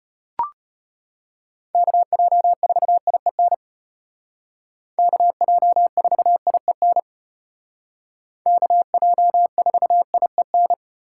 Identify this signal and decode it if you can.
Morse code, heard as a keyed tone, about 33 wpm, 700 Hz. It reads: KJ4IEN KJ4IEN KJ4IEN